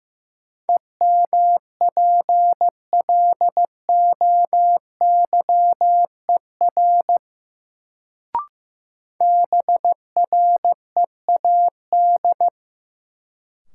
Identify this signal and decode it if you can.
Morse code, heard as a keyed tone, about 15 wpm, 700 Hz. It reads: EMPLOYER BREAD
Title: Sentences from Top 1600 Words - New Word Drill Alt - 15wpm